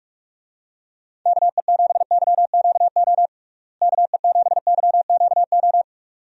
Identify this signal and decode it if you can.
Morse code, heard as a keyed tone, about 46 wpm, 700 Hz. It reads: KE7YQO KE7YQO
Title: USA Call Signs - 6 Characters - Repeated 2x - 45wpm